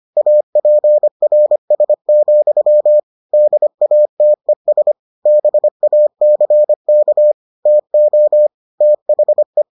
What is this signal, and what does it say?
Morse code, heard as a keyed tone, about 25 wpm, 600 Hz. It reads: APRS, DATES BACK TO THE